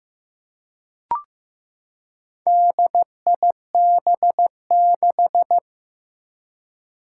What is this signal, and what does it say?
Morse code, heard as a keyed tone, about 15 wpm, 700 Hz. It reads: DIB6
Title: Four Challenging Letter-Number Combinations 15wpm